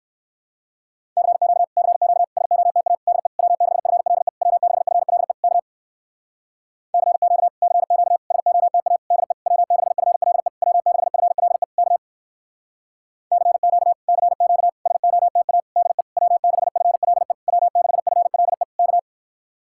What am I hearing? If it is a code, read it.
Morse code, heard as a keyed tone, about 58 wpm, 700 Hz. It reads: CQ CQ SOTA DE W6FLE W6FLE K CQ CQ SOTA DE W6FLE W6FLE K CQ CQ SOTA DE W6FLE W6FLE K